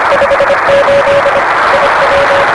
Morse code, heard as a keyed tone, about 25 wpm, 560 Hz, with noise. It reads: 58 IW
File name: RS-7 Beacon 16 Feb 1985 oz6bl